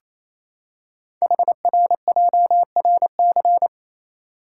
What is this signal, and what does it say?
Morse code, heard as a keyed tone, about 28 wpm, 700 Hz. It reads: HRJRC
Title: Code Groups - 5 Letters - 28wpm